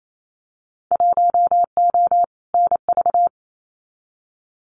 Morse code, heard as a keyed tone, about 28 wpm, 700 Hz. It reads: 1O NV
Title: Winter Field Day Exchanges Encore 28wpm